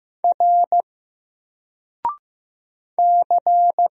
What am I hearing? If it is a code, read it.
Morse code, heard as a keyed tone, about 15 wpm, 700 Hz. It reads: R C
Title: Single Character (Letter:Number:Punctuation:Prosign) 15wpm